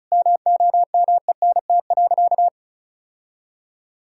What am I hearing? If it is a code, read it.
Morse code, heard as a keyed tone, about 34 wpm, 700 Hz. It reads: MOMENT.
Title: Sentences from Top 1000 Words - Review Part 6 - 35wpm